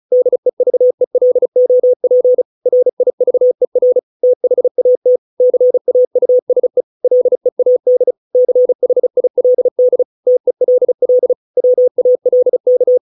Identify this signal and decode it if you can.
Morse code, heard as a keyed tone, about 34 wpm, 500 Hz. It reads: DEVELOP RIVER THAT CAUSE LEAD CHILD TELL WALK